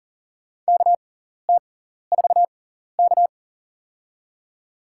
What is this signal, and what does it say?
Morse code, heard as a keyed tone, about 40 wpm, spaced as for 14 wpm, 700 Hz. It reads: KT4K
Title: USA Call Signs - 4 Characters - 6x Character Spacing - 40wpm